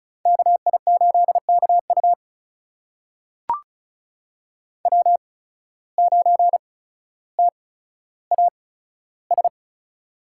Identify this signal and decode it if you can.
Morse code, heard as a keyed tone, about 36 wpm, spaced as for 15 wpm, 700 Hz. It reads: KI8KU W9TAS
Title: USA Call Signs - 5 Characters - 8x Character Spacing - 35wpm